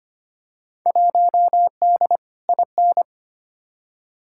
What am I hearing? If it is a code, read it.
Morse code, heard as a keyed tone, about 25 wpm, 700 Hz. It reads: 1D IN
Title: Field Day Exchanges - 25wpm